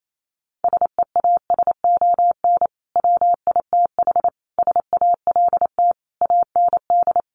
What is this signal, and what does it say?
Morse code, heard as a keyed tone, about 28 wpm, 700 Hz. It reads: SEASON WITH SALT AND
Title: Sets of 5 Words 28wpm